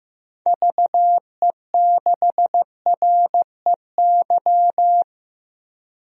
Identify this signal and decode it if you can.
Morse code, heard as a keyed tone, about 15 wpm, 700 Hz. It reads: VE6REY
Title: DXCC-100 Call Signs - 6 Characters - Repeated 3x - 15wpm